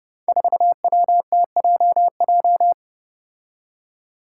Morse code, heard as a keyed tone, about 30 wpm, 700 Hz. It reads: 4WTJJ
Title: Five Challenging Letter-Number Combinations 30wpm